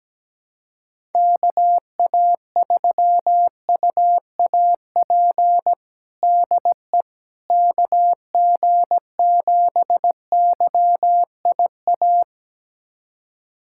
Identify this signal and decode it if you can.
Morse code, heard as a keyed tone, about 17 wpm, 700 Hz. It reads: KA3UAP DE KG7YIA
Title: USA Callsign DE Callsign - 6 Characters - 17wpm